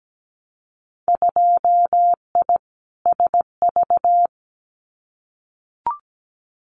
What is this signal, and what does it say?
Morse code, heard as a keyed tone, about 17 wpm, 700 Hz. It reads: 2I SV
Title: Winter Field Day Exchanges Encore 17wpm